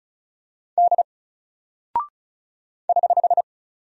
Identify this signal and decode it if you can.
Morse code, heard as a keyed tone, about 35 wpm, 700 Hz. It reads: D <HH>
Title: Single Character (Letter:Number:Punctuation:Prosign) Alt 35wpm